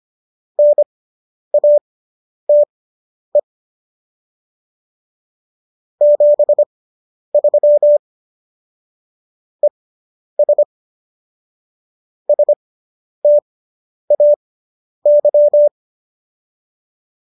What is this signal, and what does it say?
Morse code, heard as a keyed tone, about 25 wpm, spaced as for 10 wpm, 590 Hz. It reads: NATE 73 ES STAY